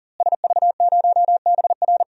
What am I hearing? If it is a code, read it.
Morse code, heard as a keyed tone, about 40 wpm, 700 Hz. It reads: SV0BR
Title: DXCC-100 Call Signs - 5 Characters - Repeated 3x - 40wpm